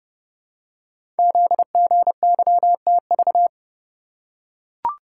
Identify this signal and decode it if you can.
Morse code, heard as a keyed tone, about 30 wpm, 700 Hz. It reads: ZGYTV
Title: Code Groups - 5 Letters - 30wpm